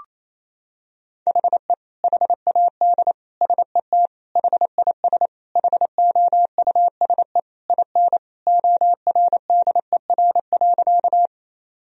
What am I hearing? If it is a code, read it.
Morse code, heard as a keyed tone, about 28 wpm, 700 Hz. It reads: HE HAD SET HIS HOUSE IN ORDER.
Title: Sentences from Top 1000 Words - Review Part 2 - 28wpm